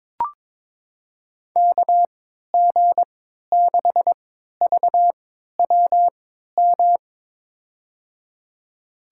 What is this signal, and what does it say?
Morse code, heard as a keyed tone, about 23 wpm, spaced as for 13 wpm, 700 Hz. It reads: KG6VWM
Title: USA Call Signs - 6 Characters - 3x Character Spacing - 22wpm